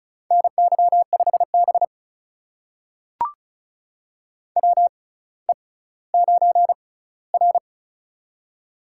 Morse code, heard as a keyed tone, about 36 wpm, spaced as for 18 wpm, 700 Hz. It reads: NY5B WE9R
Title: USA Call Signs - 4 Characters - 6x Character Spacing - 35wpm